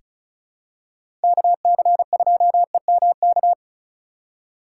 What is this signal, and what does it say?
Morse code, heard as a keyed tone, about 36 wpm, 700 Hz. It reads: KC2EMK
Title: USA Call Signs - 6 Characters - 3x Character Spacing - 35wpm